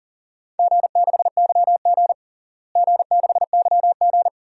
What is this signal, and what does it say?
Morse code, heard as a keyed tone, about 40 wpm, 700 Hz. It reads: G6YG G6YG
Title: DXCC-100 Call Signs - 4 Characters - Repeated 2x - 40wpm